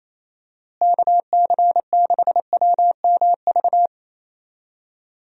Morse code, heard as a keyed tone, about 28 wpm, 700 Hz. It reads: KC6WMV
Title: USA Call Signs - 6 Characters - Repeated 3x - 28wpm